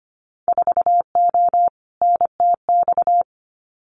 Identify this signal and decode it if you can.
Morse code, heard as a keyed tone, about 25 wpm, 700 Hz. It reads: 4O NTX
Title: Winter Field Day Exchanges 25wpm